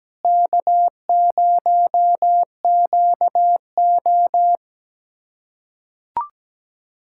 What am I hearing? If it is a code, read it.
Morse code, heard as a keyed tone, about 17 wpm, 700 Hz. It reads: K0QO
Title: USA Callsign DE Callsign - 4 Characters - 17wpm